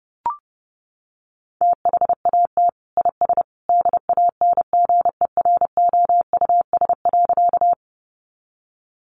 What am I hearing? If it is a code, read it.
Morse code, heard as a keyed tone, about 30 wpm, 700 Hz. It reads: THAT IS DANGEROUS.
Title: Sentences from Top 1200 Words 30wpm